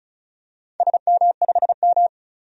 Morse code, heard as a keyed tone, about 34 wpm, 700 Hz. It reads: SM5M
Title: DXCC-100 Call Signs - 4 Characters - Repeated 3x - 35wpm